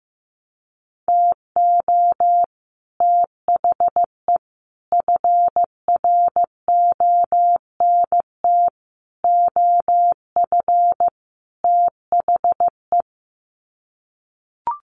Morse code, heard as a keyed tone, about 15 wpm, 700 Hz. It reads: TO THE FRONT OF THE
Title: Sets of 5 Words 15wpm